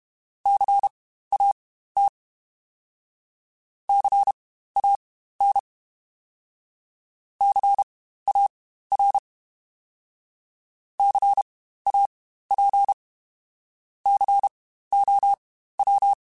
Morse code, heard as a keyed tone, about 32 wpm, spaced as for 15 wpm, 800 Hz. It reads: CAT CAN CAR CAP COW
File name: Spaced Alphas / Spaced Alphas mp3s / Cs sp @30